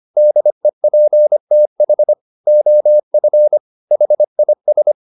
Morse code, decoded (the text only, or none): DEPTH OF HIS